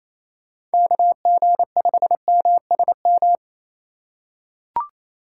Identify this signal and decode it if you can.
Morse code, heard as a keyed tone, about 28 wpm, 700 Hz. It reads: KG5MSM
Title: USA Call Signs - 6 Characters - 2x Character Spacing - 28wpm